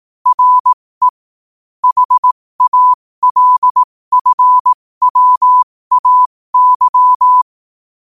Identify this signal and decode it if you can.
Morse code, heard as a keyed tone, about 18 wpm, spaced as for 15 wpm, 1000 Hz. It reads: RE HALFWAY
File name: qotdfiles / 15wpm